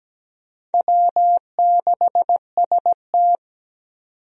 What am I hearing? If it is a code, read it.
Morse code, heard as a keyed tone, about 17 wpm, 700 Hz. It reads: W6ST
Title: USA Call Signs - 4 Characters - 17wpm